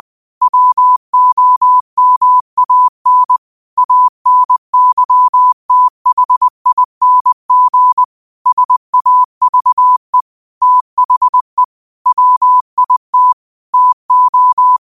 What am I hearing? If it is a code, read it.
Morse code, heard as a keyed tone, about 20 wpm, 1000 Hz. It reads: WOMAN ANYTHING SAVE THE WIT TO
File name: qotdfiles / 20wpm